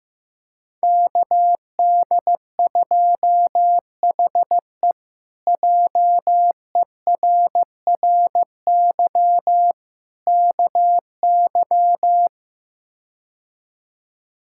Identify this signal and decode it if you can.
Morse code, heard as a keyed tone, about 15 wpm, 700 Hz. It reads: KD2HE JERRY KY
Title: K1USN SST QSO Practice - 15wpm